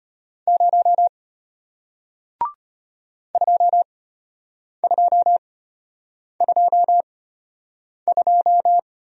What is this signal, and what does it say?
Morse code, heard as a keyed tone, about 33 wpm, spaced as for 8 wpm, 700 Hz. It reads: 0 2222